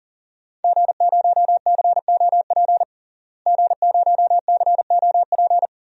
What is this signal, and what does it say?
Morse code, heard as a keyed tone, about 40 wpm, 700 Hz. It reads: G0COP G0COP
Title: DXCC-100 Call Signs - 5 Characters - Repeated 3x - 40wpm